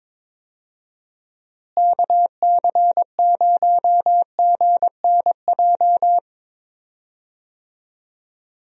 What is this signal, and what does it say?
Morse code, heard as a keyed tone, about 22 wpm, 700 Hz. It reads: KC0GNJ
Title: USA Call Signs - 6 Characters - Repeated 3x - 22wpm